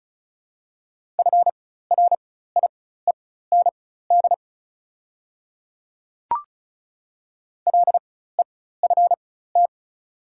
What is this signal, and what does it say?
Morse code, heard as a keyed tone, about 34 wpm, spaced as for 16 wpm, 700 Hz. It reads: FRIEND LEFT
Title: 4x Character Spacing - Top 300 Words - 35wpm